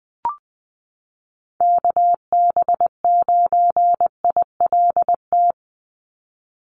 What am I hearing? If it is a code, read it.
Morse code, heard as a keyed tone, about 20 wpm, 700 Hz. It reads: KB9ILT
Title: NA SOTA Activator Callsigns 20wpm